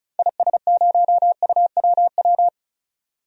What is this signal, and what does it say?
Morse code, heard as a keyed tone, about 34 wpm, 700 Hz. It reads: IS0UWW
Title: DXCC-100 Call Signs - 6 Characters - 35wpm